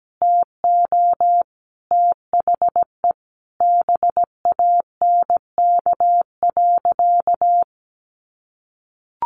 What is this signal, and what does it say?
Morse code, heard as a keyed tone, about 17 wpm, 700 Hz. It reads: TO THE BANK.
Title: Sentences from Top 1300 Words 17wpm